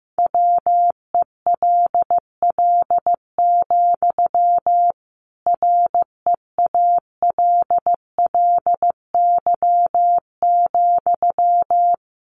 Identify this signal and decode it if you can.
Morse code, heard as a keyed tone, about 14 wpm, 700 Hz. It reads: WELL, REALLY,
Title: Sentences from Top 600 Words 15wpm